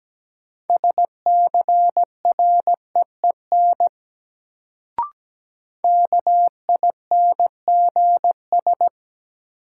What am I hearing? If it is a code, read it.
Morse code, heard as a keyed tone, about 17 wpm, 700 Hz. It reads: SCREEN KINGS
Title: Sentences from Top 1000 Words - Review Part 6 - New Word Drill - 17wpm